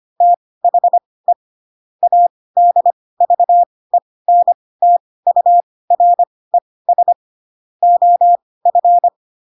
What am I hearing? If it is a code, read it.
Morse code, heard as a keyed tone, about 25 wpm, spaced as for 18 wpm, 700 Hz. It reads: THE ADVENTURES OF
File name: Pinocchio / Pinocchio ch2 12 25 ik1hge